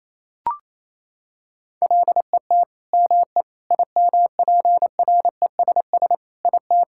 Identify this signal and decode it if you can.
Morse code, heard as a keyed tone, about 28 wpm, 700 Hz. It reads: LET ME IMPRESS IT
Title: Sentences from Top 1600 Words 28wpm